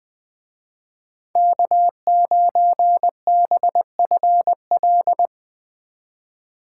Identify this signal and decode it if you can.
Morse code, heard as a keyed tone, about 22 wpm, 700 Hz. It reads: K9BFL
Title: Speed Racing - USA 1x3 Call Signs 20wpm